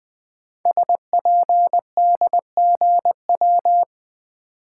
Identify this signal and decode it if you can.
Morse code, heard as a keyed tone, about 20 wpm, 700 Hz. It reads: SPDGW
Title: Code Groups - 5 Letters - 20wpm